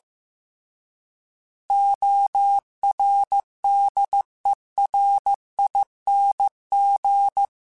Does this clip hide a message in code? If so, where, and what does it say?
Yes: Morse code from 1.7 to 7.5 s, heard as a keyed tone, about 15 wpm, 800 Hz. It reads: ORDERING